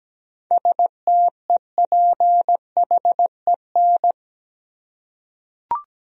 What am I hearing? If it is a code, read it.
Morse code, heard as a keyed tone, about 17 wpm, 700 Hz. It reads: STEPHEN
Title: USA Common Names - 17wpm